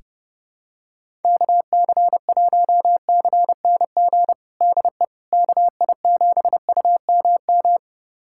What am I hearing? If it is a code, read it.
Morse code, heard as a keyed tone, about 30 wpm, 700 Hz. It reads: KC1CNG DE KI7UMM
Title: USA Callsign DE Callsign 30wpm